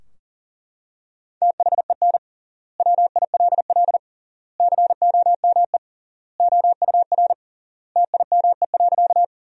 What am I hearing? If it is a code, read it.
Morse code, heard as a keyed tone, about 40 wpm, 700 Hz. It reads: THEN WILL COME OUR TIME.
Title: 3x Word-Spacing - Sentences from Top 200 Words - 40wpm